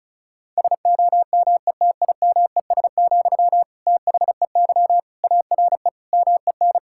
Morse code, heard as a keyed tone, about 34 wpm, 700 Hz. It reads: SOMETIMES, THEY ARE MEN
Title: Sentences from Top 1000 Words - Review Part 5 - 35wpm